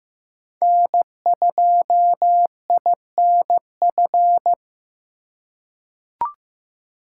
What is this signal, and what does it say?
Morse code, heard as a keyed tone, about 15 wpm, 700 Hz. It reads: N2INF
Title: USA Call Signs - 5 Characters - 2x Character Spacing - 15wpm